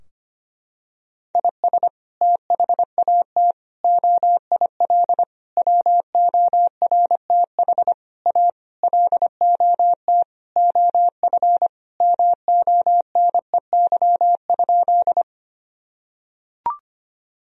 Morse code, heard as a keyed tone, about 25 wpm, 700 Hz. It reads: IS THAT OIL WORTH A LOT OF MONEY?
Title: Sentences from Top 1000 Words - Review Part 4 - 25wpm